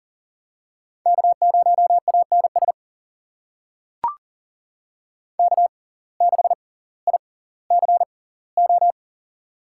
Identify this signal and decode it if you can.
Morse code, heard as a keyed tone, about 40 wpm, spaced as for 20 wpm, 700 Hz. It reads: K0ANS K6ICO
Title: USA Call Signs - 5 Characters - 6x Character Spacing - 40wpm